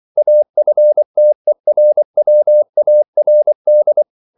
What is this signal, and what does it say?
Morse code, heard as a keyed tone, about 24 wpm, 600 Hz. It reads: AFTERWARD